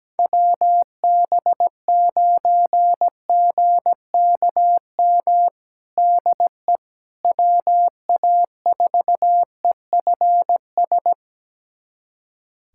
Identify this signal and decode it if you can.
Morse code, heard as a keyed tone, about 17 wpm, 700 Hz. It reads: WB9GKM DE WA4EFS
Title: USA Callsign DE Callsign - 6 Characters - 17wpm